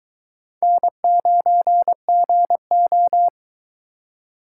Morse code, heard as a keyed tone, about 22 wpm, 700 Hz. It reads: N9GO